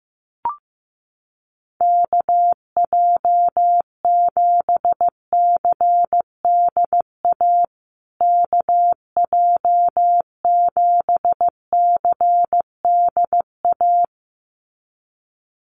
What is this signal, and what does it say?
Morse code, heard as a keyed tone, about 15 wpm, 700 Hz. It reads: KJ7CDA KJ7CDA
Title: USA Call Signs Repeated 2x 15wpm